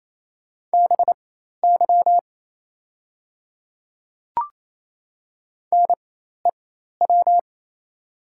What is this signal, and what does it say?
Morse code, heard as a keyed tone, about 28 wpm, spaced as for 13 wpm, 700 Hz. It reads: BY NEW
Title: 4x Character Spacing - Top 300 Words - 28wpm